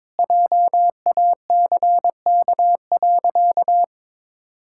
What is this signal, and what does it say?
Morse code, heard as a keyed tone, about 22 wpm, 700 Hz. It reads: JACK.
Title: Pangram Challenge v2 - 22wpm